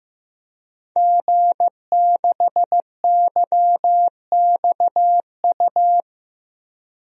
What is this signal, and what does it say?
Morse code, heard as a keyed tone, about 15 wpm, 700 Hz. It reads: G6YXU
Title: DXCC-100 Call Signs - 5 Characters - 15wpm